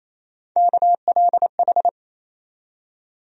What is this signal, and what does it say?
Morse code, heard as a keyed tone, about 28 wpm, 700 Hz. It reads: KLH
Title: USA Call Sign Postfixes - 3 Letters 28wpm